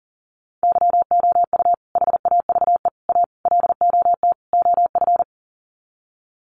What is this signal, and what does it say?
Morse code, heard as a keyed tone, about 40 wpm, 700 Hz. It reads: YOU HAVE A LOT OF
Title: Sets of 5 Words Encore 40wpm